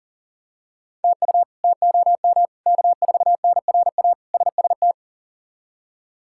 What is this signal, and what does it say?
Morse code, heard as a keyed tone, about 40 wpm, 700 Hz. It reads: TU TOM K4NRA SST
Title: K1USN SST QSO Practice - 40wpm